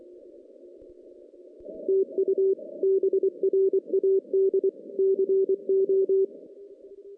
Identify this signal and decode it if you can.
Morse code, heard as a keyed tone, about 21 wpm, 410 Hz. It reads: TU BRAD CO
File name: 2016 JAN CW NAQP / 2016 Jan naqp k5ot - 0-10